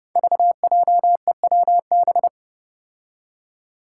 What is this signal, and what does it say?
Morse code, heard as a keyed tone, about 30 wpm, 700 Hz. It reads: VJEWB
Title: Five Challenging Letter-Number Combinations 30wpm